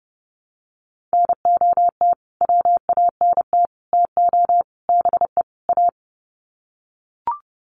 Morse code, heard as a keyed tone, about 30 wpm, 700 Hz. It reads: NOT WANT TO BE A
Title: Sets of 5 Words Encore 30wpm